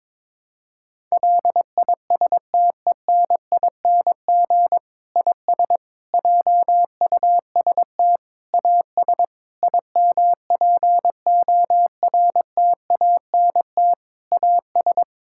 Morse code, heard as a keyed tone, about 22 wpm, 700 Hz. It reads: LISTENING IS JUST AS IMPORTANT AS
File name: Difficult Sentences with Hints - 22wpm